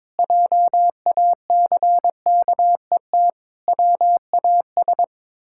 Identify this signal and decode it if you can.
Morse code, heard as a keyed tone, about 22 wpm, 700 Hz. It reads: JACKET WAS
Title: Pangram Challenge v2 - 22wpm